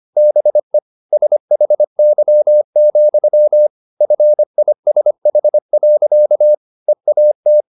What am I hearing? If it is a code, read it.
Morse code, heard as a keyed tone, about 25 wpm, 600 Hz. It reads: BE SHY, FISH. EAT
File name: the old man and the sea part 1 25wpm